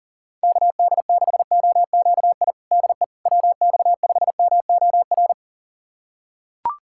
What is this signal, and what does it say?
Morse code, heard as a keyed tone, about 40 wpm, 700 Hz. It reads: KD6OQI DE WX5MOR
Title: USA Callsign DE Callsign - 6 Characters - 40wpm